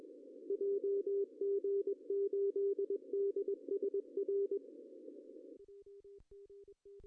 Morse code, heard as a keyed tone, about 22 wpm, 410 Hz, with radio noise. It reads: JG8DSR
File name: CW / cqww-cw-k5tr